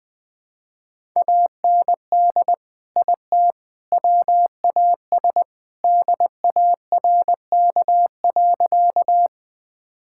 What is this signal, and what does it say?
Morse code, heard as a keyed tone, about 20 wpm, 700 Hz. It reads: AND IT WAS DARK.
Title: Sentences with 3-Word Phrases - 20wpm